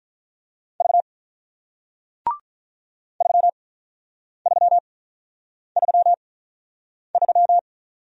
Morse code, heard as a keyed tone, about 48 wpm, spaced as for 9 wpm, 700 Hz. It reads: V 3333